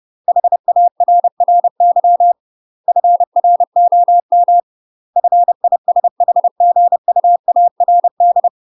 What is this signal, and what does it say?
Morse code, heard as a keyed tone, about 30 wpm, 700 Hz. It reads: HARRY FROM FISHGUARD
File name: UK MASTER QSOs TRAINING / 30WPM / UK MASTER QSOs 0008